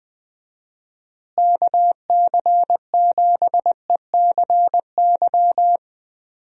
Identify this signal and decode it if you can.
Morse code, heard as a keyed tone, about 20 wpm, 700 Hz. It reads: KC7ECY
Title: USA Call Signs - 6 Characters - Repeated 3x - 20wpm